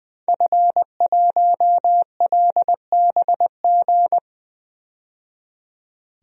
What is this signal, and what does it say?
Morse code, heard as a keyed tone, about 21 wpm, 700 Hz. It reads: F1LBG
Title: DXCC-100 Call Signs - 5 Characters - 20wpm